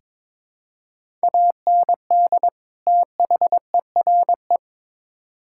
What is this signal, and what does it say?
Morse code, heard as a keyed tone, about 22 wpm, 700 Hz. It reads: AND THERE
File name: Build A Phrase - 2-Word Phrases - 22wpm